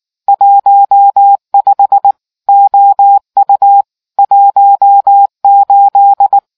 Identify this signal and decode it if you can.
Morse code, heard as a keyed tone, about 19 wpm, 800 Hz. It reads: 15 OU 18